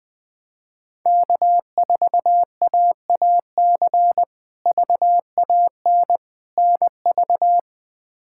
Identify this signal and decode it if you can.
Morse code, heard as a keyed tone, about 21 wpm, 700 Hz. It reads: K4AAC VAN NV
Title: K1USN SST QSO Practice - 20wpm